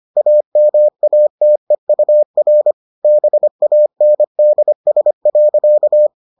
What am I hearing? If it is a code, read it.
Morse code, heard as a keyed tone, about 25 wpm, 600 Hz. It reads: AMATEUR BANDS.